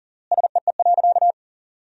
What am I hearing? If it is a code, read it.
Morse code, heard as a keyed tone, about 40 wpm, 700 Hz. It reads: SEE.
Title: 3x Word-Spacing - Sentences from Top 100 Words - 40wpm